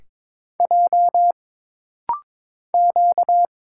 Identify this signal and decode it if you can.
Morse code, heard as a keyed tone, about 22 wpm, 700 Hz. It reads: J Q
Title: Single Letters - Rapid-Fire 22wpm